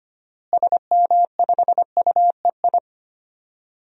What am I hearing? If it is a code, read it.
Morse code, heard as a keyed tone, about 25 wpm, 700 Hz. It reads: SM5UEI
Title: DXCC-100 Call Signs - 6 Characters - Repeated 2x - 25wpm